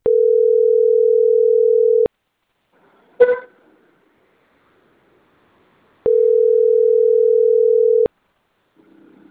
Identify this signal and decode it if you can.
Ringback tone (North America), 440 + 480 Hz.